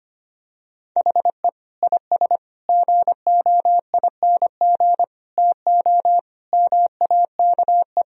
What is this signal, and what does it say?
Morse code, heard as a keyed tone, about 25 wpm, 700 Hz. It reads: HE IS GOING TO MAKE
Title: Sentences from Top 1000 Words - Review Part 4 - 25wpm